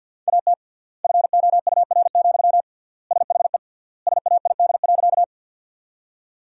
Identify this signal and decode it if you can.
Morse code, heard as a keyed tone, about 50 wpm, 700 Hz. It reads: AT FOUR, SHE SAID.
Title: 3x Word-Spacing - Sentences from Top 300 Words - 50wpm